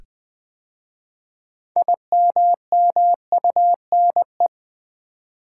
Morse code, heard as a keyed tone, about 20 wpm, 700 Hz. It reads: IMMUNE
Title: 6 Letter Words - First 3 Letters Given - 20wpm